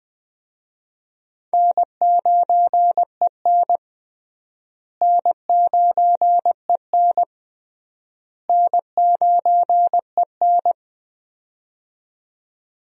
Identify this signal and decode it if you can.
Morse code, heard as a keyed tone, about 20 wpm, 700 Hz. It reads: N9EN N9EN N9EN
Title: USA Call Signs - 4 Characters - Repeated 3x - 20wpm